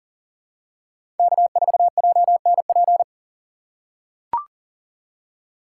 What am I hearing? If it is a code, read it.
Morse code, heard as a keyed tone, about 40 wpm, 700 Hz. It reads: K4JNP
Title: USA Call Signs - 2x Character Spacing - 40wpm